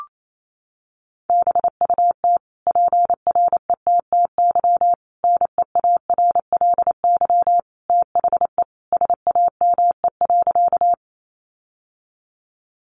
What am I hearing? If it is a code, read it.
Morse code, heard as a keyed tone, about 28 wpm, 700 Hz. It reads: BUT PRETTY NEARLY THE SAME.
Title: Sentences from Top 1300 Words 28wpm